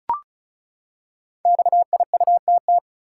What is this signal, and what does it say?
Morse code, heard as a keyed tone, about 36 wpm, 700 Hz. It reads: XIUTT
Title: Code Groups - 5 Letters - 35wpm